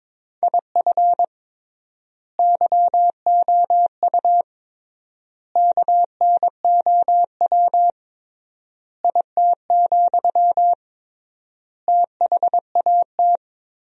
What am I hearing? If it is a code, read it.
Morse code, heard as a keyed tone, about 22 wpm, 700 Hz. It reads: IF YOU KNOW IT, THAT
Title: 3x Word-Spacing - Sentences from Top 400 Words - 22wpm